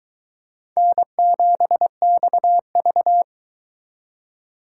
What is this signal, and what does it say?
Morse code, heard as a keyed tone, about 20 wpm, 700 Hz. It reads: N7XV
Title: Speed Racing - USA Call Signs 17wpm